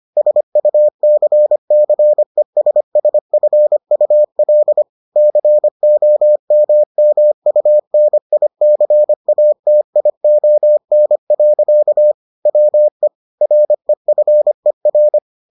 Morse code, heard as a keyed tone, about 25 wpm, 600 Hz. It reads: SUCCESSFUL COMMUNICATION. WE REFER